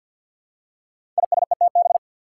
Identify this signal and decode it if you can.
Morse code, heard as a keyed tone, about 52 wpm, 700 Hz. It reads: ISETB
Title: Five Challenging Letter-Number Combinations 50wpm